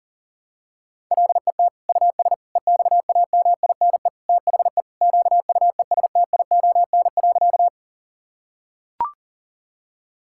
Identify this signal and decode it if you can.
Morse code, heard as a keyed tone, about 40 wpm, 700 Hz. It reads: LET US EXAMINE THE QUESTION.
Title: Sentences from Top 1600 Words 40wpm